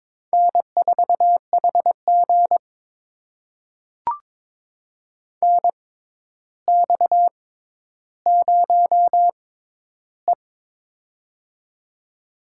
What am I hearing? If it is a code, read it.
Morse code, heard as a keyed tone, about 22 wpm, spaced as for 11 wpm, 700 Hz. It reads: N4HG NX0E
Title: USA Call Signs - 4 Characters - 6x Character Spacing - 22wpm